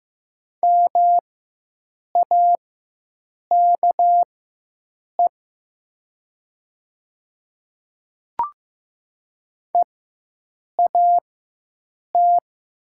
Morse code, heard as a keyed tone, about 15 wpm, spaced as for 7 wpm, 700 Hz. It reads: MAKE EAT
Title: 4x Character Spacing - Top 300 Words - 15wpm